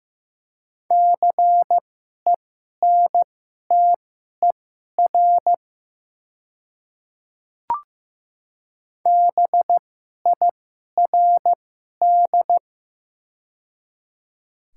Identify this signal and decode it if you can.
Morse code, heard as a keyed tone, about 15 wpm, spaced as for 11 wpm, 700 Hz. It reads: CENTER BIRD
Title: 2x Character Spacing - Top 500 Words - 15wpm